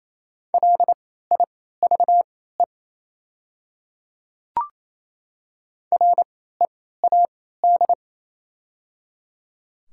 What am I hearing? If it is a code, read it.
Morse code, heard as a keyed tone, about 28 wpm, spaced as for 16 wpm, 700 Hz. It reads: LIVE READ
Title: 3x Character Spacing - Top 300 Words - 28wpm